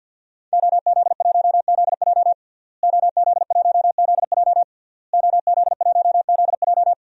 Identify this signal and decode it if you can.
Morse code, heard as a keyed tone, about 50 wpm, 700 Hz. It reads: OZ1ZJ OZ1ZJ OZ1ZJ
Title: DXCC-100 Call Signs - 5 Characters - Repeated 3x - 50wpm